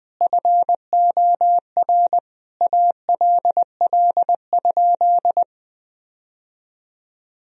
Morse code, heard as a keyed tone, about 19 wpm, 700 Hz. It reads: FOR ALL?
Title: Sentences from Top 1000 Words - Review Part 3 - 20wpm